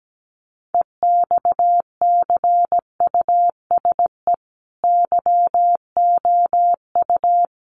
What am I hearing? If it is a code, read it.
Morse code, heard as a keyed tone, about 17 wpm, 700 Hz. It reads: EXCUSE YOU
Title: Sentences from Top 700 Words 17wpm